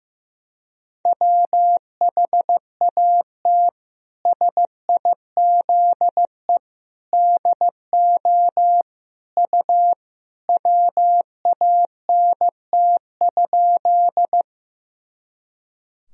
Morse code, heard as a keyed tone, about 15 wpm, 700 Hz. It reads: WHAT SIZE DO U WANT?
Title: Sentences from Top 400 Words and Abbreviations 15wpm